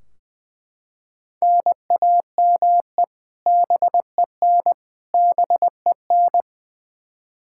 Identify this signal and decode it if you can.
Morse code, heard as a keyed tone, about 20 wpm, 700 Hz. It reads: NAME BEN BEN
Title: K1USN SST QSO Practice 20wpm